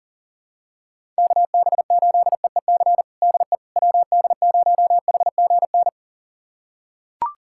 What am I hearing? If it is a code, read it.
Morse code, heard as a keyed tone, about 40 wpm, 700 Hz. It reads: KB8EEC DE WD0HGN
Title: USA Callsign DE Callsign - 6 Characters - 40wpm